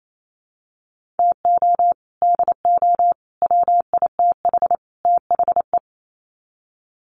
Morse code, heard as a keyed tone, about 28 wpm, 700 Hz. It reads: TO DO WITH THE
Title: Sets of 4 Words 28wpm